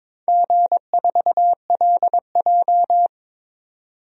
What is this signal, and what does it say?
Morse code, heard as a keyed tone, about 22 wpm, 700 Hz. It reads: G4LJ
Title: DXCC-100 Call Signs - 4 Characters - 22wpm